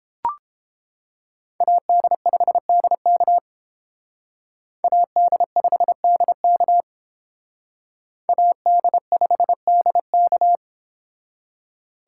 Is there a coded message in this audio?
Yes: Morse code from 1.6 to 10.6 s, heard as a keyed tone, about 29 wpm, 700 Hz. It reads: AD5DK AD5DK AD5DK